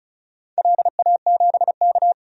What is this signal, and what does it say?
Morse code, heard as a keyed tone, about 36 wpm, 690 Hz. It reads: LA7K
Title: DXCC-100 Call Signs - 4 Characters - 35wpm